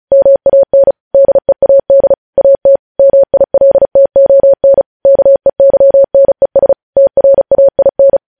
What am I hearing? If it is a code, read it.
Morse code, heard as a keyed tone, about 36 wpm, 550 Hz. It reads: MAN DEAD AT MILTON KEYNES TRAIN
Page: news headlines in morse code at 35 wpm